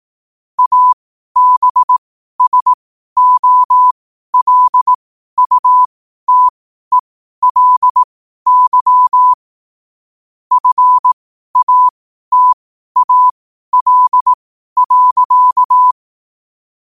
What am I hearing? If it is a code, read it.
Morse code, heard as a keyed tone, about 18 wpm, spaced as for 12 wpm, 1000 Hz. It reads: ABSOLUTELY FATAL.
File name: qotdfiles / 12wpm